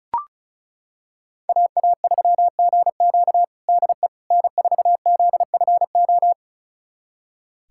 Morse code, heard as a keyed tone, about 36 wpm, 700 Hz. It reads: AA3GQ DE N4ZFO
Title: USA Callsign DE Callsign - 5 Characters - 35wpm